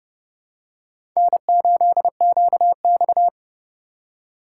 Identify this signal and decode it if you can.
Morse code, heard as a keyed tone, about 32 wpm, 700 Hz. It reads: N8QX
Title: USA Call Signs - 4 Characters - 2x Character Spacing - 30wpm